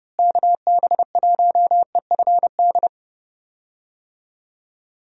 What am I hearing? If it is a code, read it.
Morse code, heard as a keyed tone, about 32 wpm, 700 Hz. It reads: KB1EFD
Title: USA Call Signs - 6 Characters - Repeated 2x - 30wpm